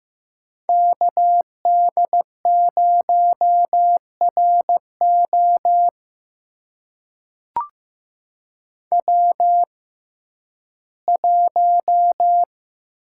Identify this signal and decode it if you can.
Morse code, heard as a keyed tone, about 15 wpm, spaced as for 11 wpm, 700 Hz. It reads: KD0RO W1
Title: USA Call Signs - 5 Characters - 6x Character Spacing - 15wpm